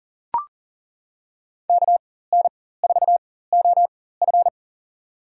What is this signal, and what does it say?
Morse code, heard as a keyed tone, about 40 wpm, spaced as for 19 wpm, 700 Hz. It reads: KN4OF
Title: USA Call Signs - 5 Characters - 4x Character Spacing - 40wpm